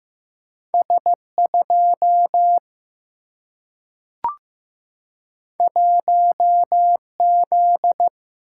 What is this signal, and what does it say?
Morse code, heard as a keyed tone, about 15 wpm, 700 Hz. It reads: S 2 1 Z
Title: Single-Letter Single-Number 15wpm